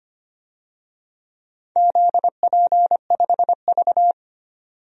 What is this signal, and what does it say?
Morse code, heard as a keyed tone, about 25 wpm, 700 Hz. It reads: ZP5V
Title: DXCC-100 Call Signs - 4 Characters - 25wpm